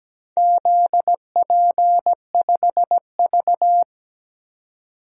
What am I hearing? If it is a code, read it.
Morse code, heard as a keyed tone, about 17 wpm, 700 Hz. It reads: ZP5V
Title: DXCC-100 Call Signs - 4 Characters - 17wpm